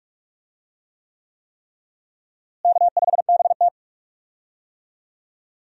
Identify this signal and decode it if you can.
Morse code, heard as a keyed tone, about 41 wpm, 700 Hz. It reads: K5BT
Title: USA Call Signs - 4 Characters - Repeated 3x - 45wpm